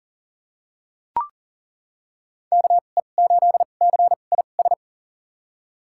Morse code, heard as a keyed tone, about 37 wpm, spaced as for 28 wpm, 700 Hz. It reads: KE8CIS
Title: USA Call Signs - 6 Characters - 2x Character Spacing - 40wpm